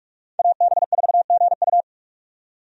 Morse code, heard as a keyed tone, about 46 wpm, 700 Hz. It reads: AB4GU
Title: USA Call Signs - 5 Characters - 45wpm